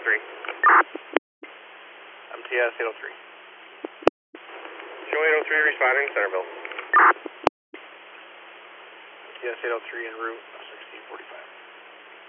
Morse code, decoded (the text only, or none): none